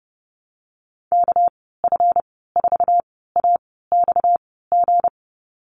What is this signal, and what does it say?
Morse code, heard as a keyed tone, about 30 wpm, spaced as for 17 wpm, 700 Hz. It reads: KF4AXG
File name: Lesson 37g - Letter x - Call Signs 30wpm